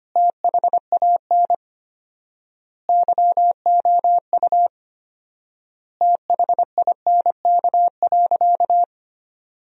Morse code, heard as a keyed tone, about 25 wpm, 700 Hz. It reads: THAN YOU THINK.
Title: 4x Word-Spacing - Sentences from Top 400 Words - 25wpm